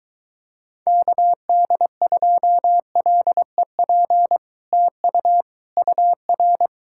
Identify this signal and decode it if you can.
Morse code, heard as a keyed tone, about 24 wpm, 700 Hz. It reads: KD2LEP TU UR